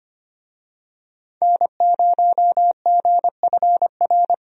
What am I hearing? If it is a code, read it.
Morse code, heard as a keyed tone, about 25 wpm, 700 Hz. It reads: N0GFR
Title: USA Call Signs - 5 Characters - 25wpm